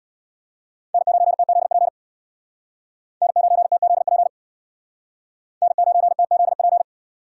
Morse code, heard as a keyed tone, about 64 wpm, 700 Hz. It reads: N9TZP N9TZP N9TZP